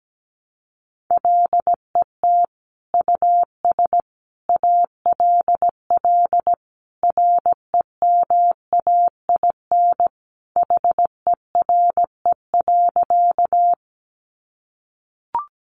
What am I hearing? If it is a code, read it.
Morse code, heard as a keyed tone, about 17 wpm, 700 Hz. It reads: LET US ALL REMAIN HERE.
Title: Sentences from Top 1200 Words 17wpm